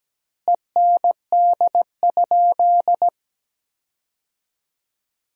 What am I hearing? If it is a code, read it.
Morse code, heard as a keyed tone, about 17 wpm, 700 Hz. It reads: END?
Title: 4x Word-Spacing - Sentences from Top 200 Words - 17wpm